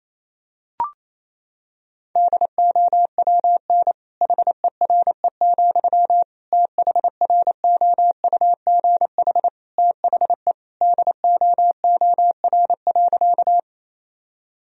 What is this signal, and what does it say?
Morse code, heard as a keyed tone, about 28 wpm, 700 Hz. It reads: DOWN HERE, THROUGH THE DOOR.
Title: Sentences from Top 1000 Words - Review Part 1 - 28wpm